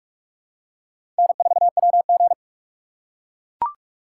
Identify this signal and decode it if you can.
Morse code, heard as a keyed tone, about 42 wpm, 700 Hz. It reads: N4WG
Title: USA Call Signs - 4 Characters - 3x Character Spacing - 45wpm